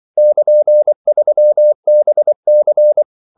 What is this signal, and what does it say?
Morse code, heard as a keyed tone, about 24 wpm, 600 Hz. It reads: <KN>3BC
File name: CW / Practice / EnglishQuotations0224